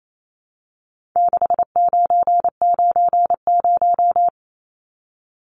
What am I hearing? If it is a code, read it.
Morse code, heard as a keyed tone, about 28 wpm, 700 Hz. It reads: 6990
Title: Numbers - 4-Digit 28wpm